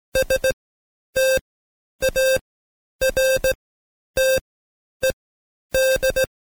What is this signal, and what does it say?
Morse code, heard as a keyed tone, about 17 wpm, spaced as for 10 wpm, 530 Hz. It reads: STARTED
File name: Code Quick / AudioDownld2 / Seg131-Stepper4 9WPM